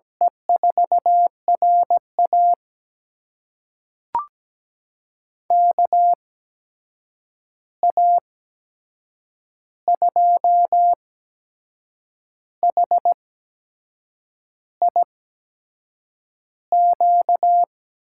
Morse code, heard as a keyed tone, about 17 wpm, spaced as for 6 wpm, 700 Hz. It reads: E4RA KA2HIQ
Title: USA Call Signs - 8x Character Spacing - 17wpm